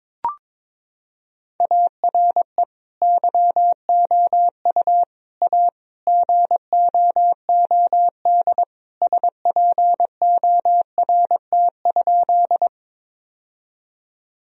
Morse code, heard as a keyed tone, about 22 wpm, 700 Hz. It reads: ARE YOU A GOOD SPORT?
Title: Sentences from Top 1000 Words - Review Part 4 - 22wpm